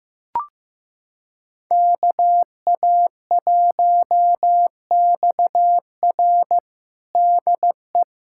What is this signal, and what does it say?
Morse code, heard as a keyed tone, about 15 wpm, 700 Hz. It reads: KA1XR DE
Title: USA Callsign DE Callsign - 5 Characters - 15wpm